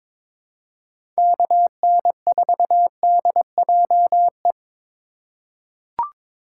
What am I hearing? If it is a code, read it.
Morse code, heard as a keyed tone, about 22 wpm, 700 Hz. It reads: KN4DJE
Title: USA Call Signs - 2x Character Spacing - 22wpm